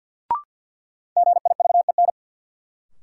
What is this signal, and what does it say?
Morse code, heard as a keyed tone, about 51 wpm, 700 Hz. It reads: GIVEN
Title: Sentences from Top 1400 Words - New Word Drill - 50wpm